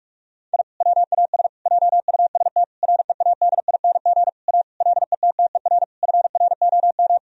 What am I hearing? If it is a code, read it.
Morse code, heard as a keyed tone, about 46 wpm, 700 Hz. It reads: I WAS JUST READING A LETTER FROM
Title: Sentences from Top 1000 Words - Review Part 3 - 45wpm